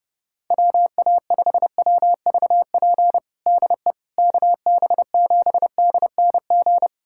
Morse code, heard as a keyed tone, about 30 wpm, 700 Hz. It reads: WA5WVP DE KB7DNG
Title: USA Callsign DE Callsign - 6 Characters - 30wpm